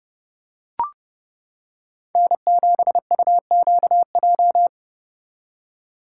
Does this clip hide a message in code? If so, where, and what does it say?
Yes: Morse code from 2.1 to 4.7 s, heard as a keyed tone, about 29 wpm, 700 Hz. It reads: N7UQJ